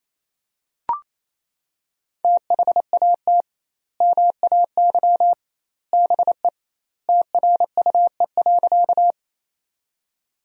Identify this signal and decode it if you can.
Morse code, heard as a keyed tone, about 28 wpm, 700 Hz. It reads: THAT MAY BE TRUE.
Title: 2x Word-Spacing - Sentences from Top 400 Words - 28wpm